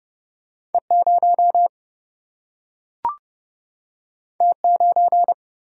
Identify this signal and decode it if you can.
Morse code, heard as a keyed tone, about 30 wpm, 700 Hz. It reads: E0 T9
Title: Single-Letter Single-Number 30wpm